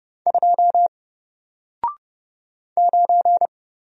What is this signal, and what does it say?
Morse code, heard as a keyed tone, about 30 wpm, 700 Hz. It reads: 2 9
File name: Single-Letter-Number-30wpm